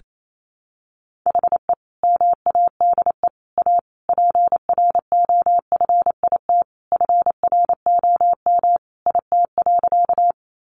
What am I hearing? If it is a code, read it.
Morse code, heard as a keyed tone, about 28 wpm, 700 Hz. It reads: HE MADE A PROFIT FROM IT.
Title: Sentences from Top 800 Words 28wpm